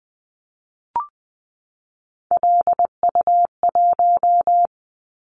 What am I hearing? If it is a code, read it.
Morse code, heard as a keyed tone, about 20 wpm, 700 Hz. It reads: LU1
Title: Callsign Prefixes for Top 100 DXCC - 20wpm